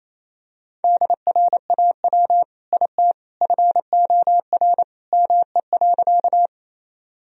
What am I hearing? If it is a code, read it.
Morse code, heard as a keyed tone, about 28 wpm, 700 Hz. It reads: DRAW IT FOR ME.
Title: Sentences from Top 1000 Words - Review Part 1 - 28wpm